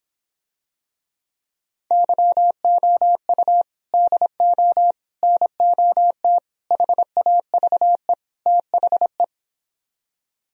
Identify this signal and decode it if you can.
Morse code, heard as a keyed tone, about 26 wpm, 700 Hz. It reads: YOU DO NOT HAVE THE